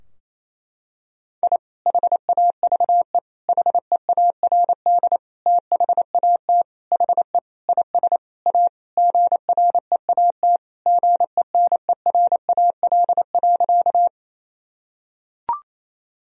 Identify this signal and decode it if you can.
Morse code, heard as a keyed tone, about 28 wpm, 700 Hz. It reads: I HAVE HEARD THAT HE IS A GREAT GENERAL.
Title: Sentences from Top 1000 Words - Review Part 3 - 28wpm